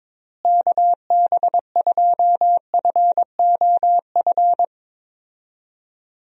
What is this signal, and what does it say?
Morse code, heard as a keyed tone, about 22 wpm, 700 Hz. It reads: KB2FOF
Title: USA Call Signs - 6 Characters - Repeated 2x - 22wpm